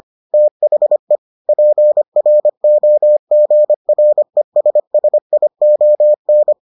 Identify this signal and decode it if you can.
Morse code, heard as a keyed tone, about 25 wpm, 600 Hz. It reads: THE PROGRESSION